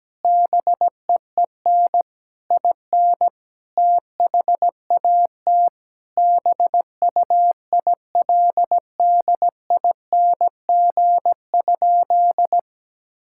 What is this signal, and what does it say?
Morse code, heard as a keyed tone, about 17 wpm, 700 Hz. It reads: BEEN IN THAT BUILDING?
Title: Sentences from Top 1000 Words - Review Part 5 - 17wpm